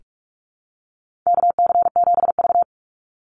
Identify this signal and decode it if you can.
Morse code, heard as a keyed tone, about 44 wpm, 700 Hz. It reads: KC7V
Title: Call Signs - 45wpm